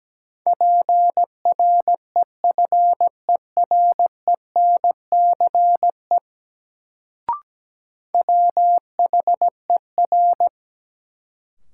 Morse code, heard as a keyed tone, about 17 wpm, 700 Hz. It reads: PREFERENCE WHER
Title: Sentences from Top 1500 Words - New Word Drill - 17wpm